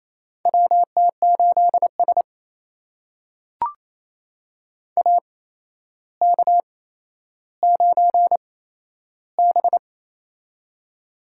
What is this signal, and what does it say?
Morse code, heard as a keyed tone, about 28 wpm, spaced as for 12 wpm, 700 Hz. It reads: WT8S AK9B
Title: USA Call Signs - 4 Characters - 8x Character Spacing - 28wpm